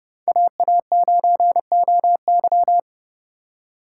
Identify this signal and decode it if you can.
Morse code, heard as a keyed tone, about 30 wpm, 700 Hz. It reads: AA9OY